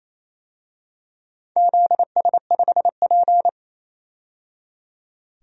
Morse code, heard as a keyed tone, about 28 wpm, 700 Hz. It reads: ZS5P
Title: DXCC-100 Call Signs - 4 Characters - 28wpm